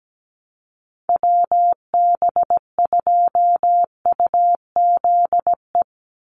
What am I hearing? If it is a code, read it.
Morse code, heard as a keyed tone, about 17 wpm, 700 Hz. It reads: WB2UZE
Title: Call Signs - 17wpm